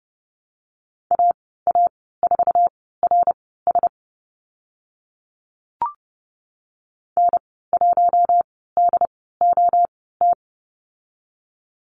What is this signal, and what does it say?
Morse code, heard as a keyed tone, about 30 wpm, spaced as for 17 wpm, 700 Hz. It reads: AA4RS N1DOT
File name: Lesson 14g - Number 2 - Call Signs 30wpm